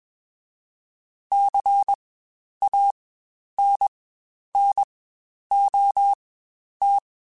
Morse code, heard as a keyed tone, about 21 wpm, spaced as for 10 wpm, 800 Hz. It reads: CANNOT
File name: Quotes Spaced / Quotes Spaced-mp3s / Advice sp @20